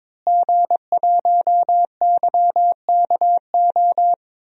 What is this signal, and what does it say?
Morse code, heard as a keyed tone, about 22 wpm, 700 Hz. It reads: G1YKO
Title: DXCC-100 Call Signs - 5 Characters - Repeated 3x - 22wpm